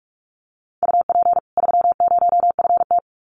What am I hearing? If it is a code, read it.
Morse code, heard as a keyed tone, about 44 wpm, 700 Hz. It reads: UP 30FT
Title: Common QSO Phrases 45wpm